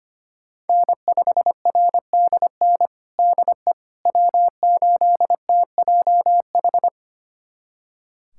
Morse code, heard as a keyed tone, about 25 wpm, 700 Hz. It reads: N5RDN DE W8TJH
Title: USA Callsign DE Callsign - 5 Characters - 25wpm